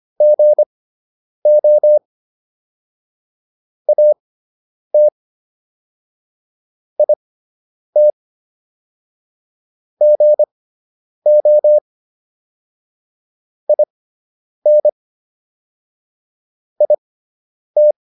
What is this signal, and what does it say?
Morse code, heard as a keyed tone, about 25 wpm, spaced as for 9 wpm, 600 Hz. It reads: GO AT IT GO IN IT